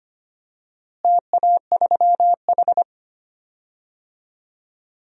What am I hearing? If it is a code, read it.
Morse code, heard as a keyed tone, about 25 wpm, 700 Hz. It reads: TA3H
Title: DXCC-100 Call Signs - 4 Characters - Repeated 2x - 25wpm